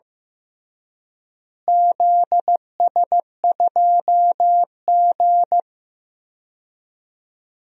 Morse code, heard as a keyed tone, about 15 wpm, 700 Hz. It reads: ZS2G
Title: DXCC-100 Call Signs - 4 Characters - Repeated 2x - 15wpm